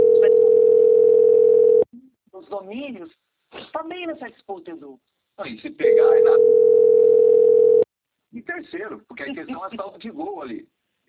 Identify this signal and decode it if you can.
Ringback tone (North America), 440 + 480 Hz, with speech.
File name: rg-1600-unknown- - - .79913